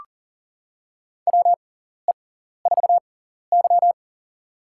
Morse code, heard as a keyed tone, about 40 wpm, spaced as for 14 wpm, 700 Hz. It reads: WE4Y
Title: USA Call Signs - 6x Character Spacing - 40wpm